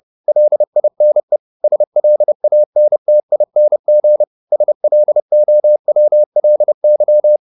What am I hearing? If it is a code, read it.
Morse code, heard as a keyed tone, about 30 wpm, 600 Hz. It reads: LINE SLANTING SLOWLY
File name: the old man and the sea / the old man and the sea 1 30wpm